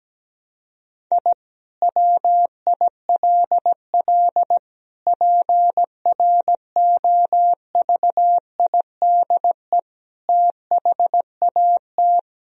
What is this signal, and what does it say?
Morse code, heard as a keyed tone, about 17 wpm, 700 Hz. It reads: I WILL PROVIDE THAT
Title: Sentences from Top 1000 Words - Review Part 3 - 17wpm